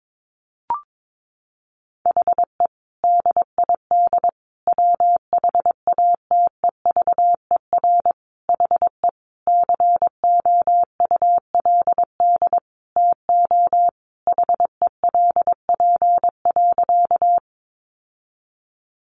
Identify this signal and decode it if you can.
Morse code, heard as a keyed tone, about 22 wpm, 700 Hz. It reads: HE DID WHATEVER HE COULD TO HELP.
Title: Sentences from Top 1100 Words 22wpm